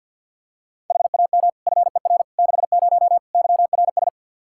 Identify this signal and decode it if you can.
Morse code, heard as a keyed tone, about 46 wpm, 700 Hz. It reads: HAM FER 60 YRS
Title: Practice QSO - Exchange 3 - More Info - 50wpm